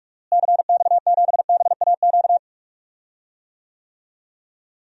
Morse code, heard as a keyed tone, about 48 wpm, 700 Hz. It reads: CX7BAQ
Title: DXCC-100 Call Signs - 6 Characters - Repeated 2x - 45wpm